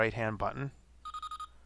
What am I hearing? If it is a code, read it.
Speech. There is no code.